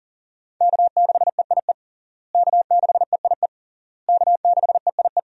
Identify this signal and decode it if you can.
Morse code, heard as a keyed tone, about 37 wpm, 700 Hz. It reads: K6EIE K6EIE K6EIE
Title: USA Call Signs - 5 Characters - Repeated 3x - 40wpm